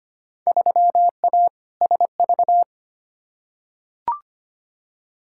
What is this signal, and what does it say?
Morse code, heard as a keyed tone, about 25 wpm, 700 Hz. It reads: 3A SV
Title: Field Day Exchanges - 25wpm